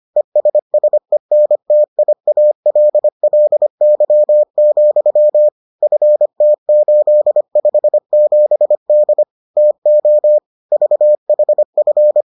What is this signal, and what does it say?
Morse code, heard as a keyed tone, about 25 wpm, 600 Hz. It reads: ESSENTIALLY, FT857D TO VHF